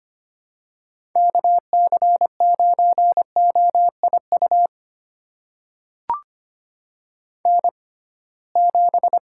Morse code, heard as a keyed tone, about 25 wpm, spaced as for 19 wpm, 700 Hz. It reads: KC9OIU N7
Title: USA Call Signs - 6x Character Spacing - 25wpm